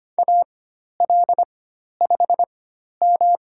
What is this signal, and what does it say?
Morse code, heard as a keyed tone, about 25 wpm, spaced as for 12 wpm, 700 Hz. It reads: AL5M
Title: USA Call Signs - 4 Characters - 4x Character Spacing - 25wpm